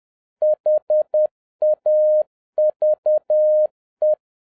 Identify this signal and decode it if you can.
Morse code, heard as a keyed tone, about 10 wpm, 610 Hz. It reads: HAVE